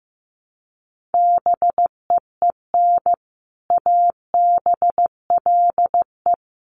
Morse code, heard as a keyed tone, about 15 wpm, 700 Hz. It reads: BEEN ABLE
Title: Sets of 3 Words 15wpm